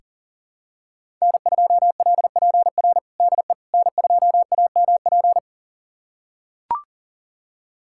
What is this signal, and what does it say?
Morse code, heard as a keyed tone, about 40 wpm, 700 Hz. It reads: N2LPR DE N2AMP
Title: USA Callsign DE Callsign - 5 Characters - 40wpm